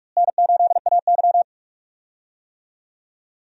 Morse code, heard as a keyed tone, about 46 wpm, 700 Hz. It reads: N8AY
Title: USA Call Signs - 4 Characters - Repeated 3x - 45wpm